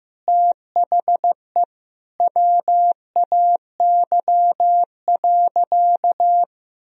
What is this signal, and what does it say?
Morse code, heard as a keyed tone, about 15 wpm, 700 Hz. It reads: THE WAY.
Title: Sentences from Top 300 Words and Abbreviations 15wpm